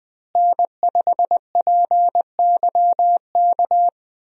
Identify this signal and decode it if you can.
Morse code, heard as a keyed tone, about 20 wpm, 700 Hz. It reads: N5PYK
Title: USA Call Signs - 5 Characters - Repeated 2x - 20wpm